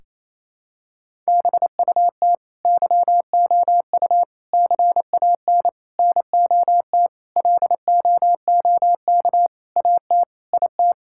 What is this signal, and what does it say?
Morse code, heard as a keyed tone, about 28 wpm, 700 Hz. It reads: BUT YOU CAN NOT LOOK AT IT
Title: Sentences from Top 1000 Words - Review Part 6 - 28wpm